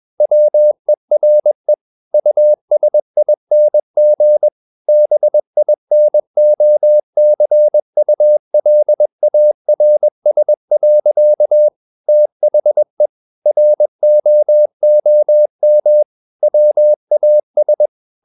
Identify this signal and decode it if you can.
Morse code, heard as a keyed tone, about 21 wpm, 600 Hz. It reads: WERE USING BINOCULARS. THE ROOM WAS
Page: CW med Gnister